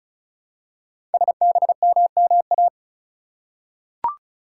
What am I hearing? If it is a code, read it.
Morse code, heard as a keyed tone, about 36 wpm, 700 Hz. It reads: SBMMA
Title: Code Groups - 5 Letters - 35wpm